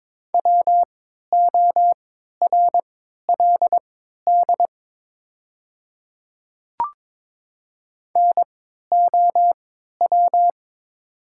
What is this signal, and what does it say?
Morse code, heard as a keyed tone, about 22 wpm, spaced as for 12 wpm, 700 Hz. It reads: WORLD NOW
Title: 3x Character Spacing - Top 500 Words - 22wpm